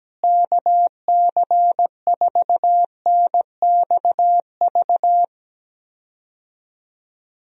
Morse code, heard as a keyed tone, about 17 wpm, 700 Hz. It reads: KC4NXV
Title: USA Call Signs - 6 Characters - Repeated 3x - 17wpm